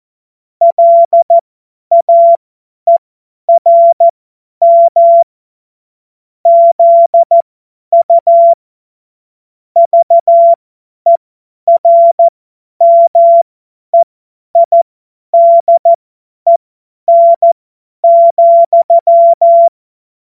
Morse code, heard as a keyed tone, about 14 wpm, spaced as for 10 wpm, 680 Hz. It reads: LAERM ZU VERMEIDEN,
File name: Morsekurs / Witze / Witz 25. Juli 2023 50 F70